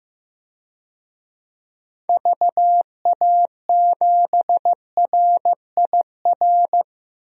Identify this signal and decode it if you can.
Morse code, heard as a keyed tone, about 15 wpm, 700 Hz. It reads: VA7RIR
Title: DXCC-100 Call Signs - 6 Characters - 15wpm